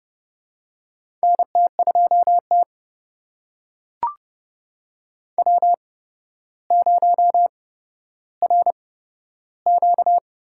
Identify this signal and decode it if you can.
Morse code, heard as a keyed tone, about 30 wpm, spaced as for 13 wpm, 700 Hz. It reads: NT2T W0RQ
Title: USA Call Signs - 8x Character Spacing - 30wpm